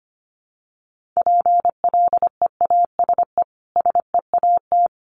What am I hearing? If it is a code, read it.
Morse code, heard as a keyed tone, about 25 wpm, 700 Hz. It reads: PLEASE SEAT
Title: Sentences from Top 1500 Words 25wpm